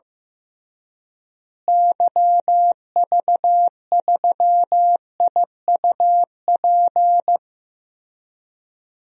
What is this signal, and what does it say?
Morse code, heard as a keyed tone, about 15 wpm, 700 Hz. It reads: YV3IUP
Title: DXCC-100 Call Signs - 6 Characters - Repeated 3x - 15wpm